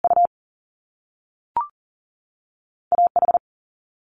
Morse code, heard as a keyed tone, about 40 wpm, 700 Hz. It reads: U A H
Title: Top Two Letter Combinations in Words 40wpm